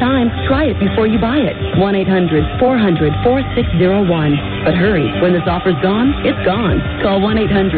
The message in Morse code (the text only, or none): none